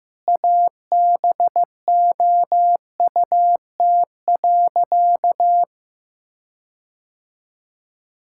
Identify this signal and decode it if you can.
Morse code, heard as a keyed tone, about 15 wpm, 700 Hz. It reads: ABOUT.
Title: Sentences from Top 1600 Words 15wpm